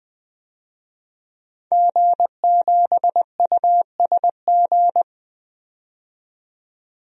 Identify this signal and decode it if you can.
Morse code, heard as a keyed tone, about 20 wpm, 700 Hz. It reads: G7USG
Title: DXCC-100 Call Signs - 5 Characters - 20wpm